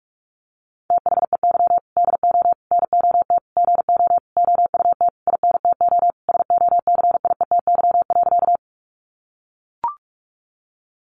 Morse code, heard as a keyed tone, about 44 wpm, 700 Hz. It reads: THEY DO NOT GO OUT INTO SOCIETY.
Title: Sentences from Top 700 Words 45wpm